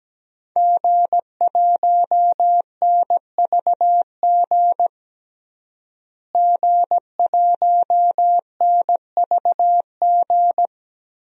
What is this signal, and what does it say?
Morse code, heard as a keyed tone, about 17 wpm, 700 Hz. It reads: G1NVG G1NVG
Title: DXCC-100 Call Signs - 5 Characters - Repeated 2x - 17wpm